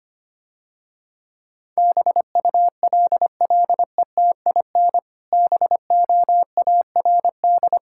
Morse code, heard as a keyed tone, about 25 wpm, 700 Hz. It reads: BULLETIN BOARD
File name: Easy Words - Speed-Racing with Hints - 25wpm